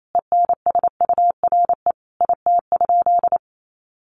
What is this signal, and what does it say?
Morse code, heard as a keyed tone, about 28 wpm, 700 Hz. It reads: ENSURE IT?
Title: Sentences from Top 1100 Words 28wpm